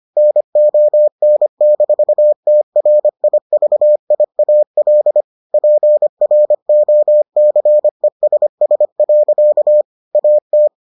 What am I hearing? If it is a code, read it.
Morse code, heard as a keyed tone, about 25 wpm, 600 Hz. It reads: NON-TRIVIAL PROCESS. AT